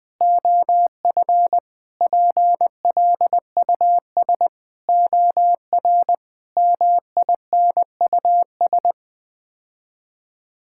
Morse code, heard as a keyed tone, about 21 wpm, 700 Hz. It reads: OF PLUS OR MINUS
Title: Sets of 8 Words - 20wpm